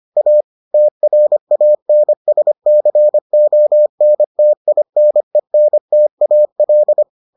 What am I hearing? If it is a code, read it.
Morse code, heard as a keyed tone, about 25 wpm, 600 Hz. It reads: A TRANSCONTINENTAL